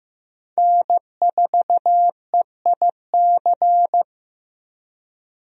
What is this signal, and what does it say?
Morse code, heard as a keyed tone, about 15 wpm, 700 Hz. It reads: N4EIC
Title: USA Call Signs - 5 Characters - 15wpm